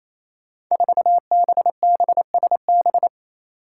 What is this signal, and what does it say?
Morse code, heard as a keyed tone, about 28 wpm, 700 Hz. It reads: 4BBSB
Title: Five Challenging Letter-Number Combinations 28wpm